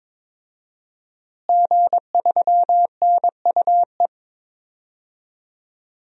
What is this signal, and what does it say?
Morse code, heard as a keyed tone, about 22 wpm, 700 Hz. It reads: G3NUE
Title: DXCC-100 Call Signs - 5 Characters - Repeated 3x - 22wpm